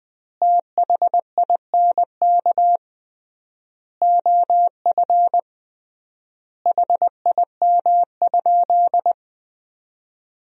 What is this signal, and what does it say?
Morse code, heard as a keyed tone, about 21 wpm, 700 Hz. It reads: THINK OF HIM?
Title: 3x Word-Spacing - Sentences from Top 200 Words - 20wpm